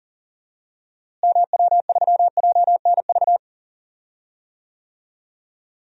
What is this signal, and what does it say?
Morse code, heard as a keyed tone, about 40 wpm, 700 Hz. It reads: MW3JNV
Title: DXCC-100 Call Signs - 6 Characters - Repeated 3x - 40wpm